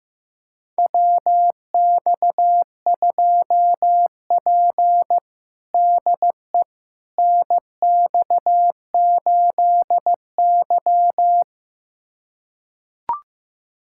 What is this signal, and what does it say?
Morse code, heard as a keyed tone, about 15 wpm, 700 Hz. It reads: WX2P DE NX8Y
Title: USA Callsign DE Callsign - 4 Characters - 15wpm